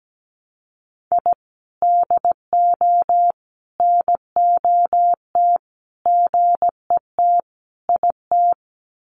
Morse code, heard as a keyed tone, about 17 wpm, 700 Hz. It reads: I DO NOT GET IT
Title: Sets of 5 Words Encore 17wpm